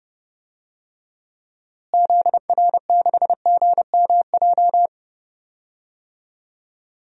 Morse code, heard as a keyed tone, about 31 wpm, 700 Hz. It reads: ZR6GMJ
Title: DXCC-100 Call Signs - 6 Characters - Repeated 3x - 30wpm